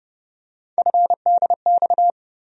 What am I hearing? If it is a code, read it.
Morse code, heard as a keyed tone, about 29 wpm, 700 Hz. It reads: FDX
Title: USA Call Sign Postfixes - 3 Letters 30wpm